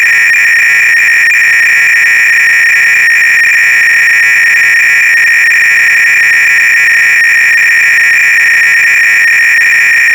300-baud modem data.
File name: ACR-Audio-Cassette-Programs / CharlesSwiger-Donations / KING cload-k